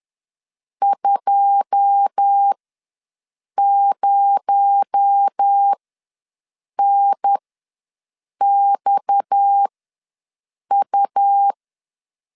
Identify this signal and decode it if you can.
Morse code, heard as a keyed tone, about 10 wpm, spaced as for 6 wpm, 800 Hz. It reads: 20NXU
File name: LEZIONI / 9 2 ws 10wpm 4130